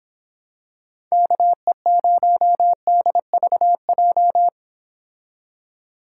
Morse code, heard as a keyed tone, about 25 wpm, 700 Hz. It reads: KE0DVJ